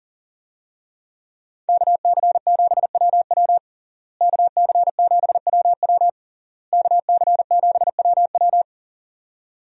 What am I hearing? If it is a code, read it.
Morse code, heard as a keyed tone, about 37 wpm, 700 Hz. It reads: KC7WW KC7WW KC7WW
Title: USA Call Signs - 5 Characters - Repeated 3x - 40wpm